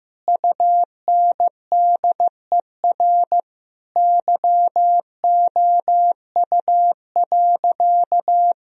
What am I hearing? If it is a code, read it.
Morse code, heard as a keyed tone, about 15 wpm, 700 Hz. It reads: UNDER YOU.
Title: Sentences from Top 1000 Words - Review Part 1 - 15wpm